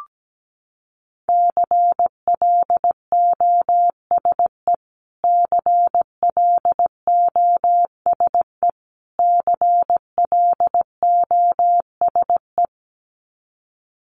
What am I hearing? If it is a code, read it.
Morse code, heard as a keyed tone, about 17 wpm, 700 Hz. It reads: CLOSE CLOSE CLOSE
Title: Top 300 Words Repeated 17wpm